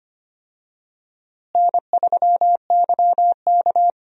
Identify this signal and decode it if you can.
Morse code, heard as a keyed tone, about 25 wpm, 700 Hz. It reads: N3YK
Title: USA Call Signs - 4 Characters - Repeated 2x - 25wpm